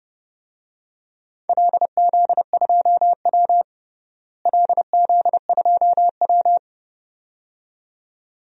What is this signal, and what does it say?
Morse code, heard as a keyed tone, about 30 wpm, 700 Hz. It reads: LZ2W LZ2W
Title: DXCC-100 Call Signs - 4 Characters - Repeated 2x - 30wpm